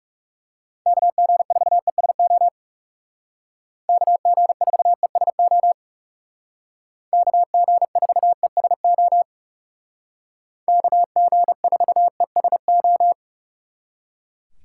Morse code, heard as a keyed tone, about 36 wpm, 700 Hz. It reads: KG4ESO KG4ESO KG4ESO KG4ESO
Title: Speed Racing - USA 2x3 Call Signs 30wpm